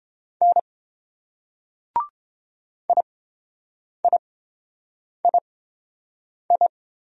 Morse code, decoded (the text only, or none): N IIII